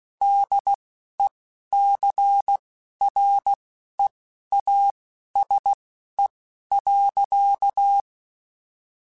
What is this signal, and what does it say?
Morse code, heard as a keyed tone, about 16 wpm, spaced as for 11 wpm, 800 Hz. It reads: DECREASE.
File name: Technical Subjects / Technical Subjects-mp3s / XC @10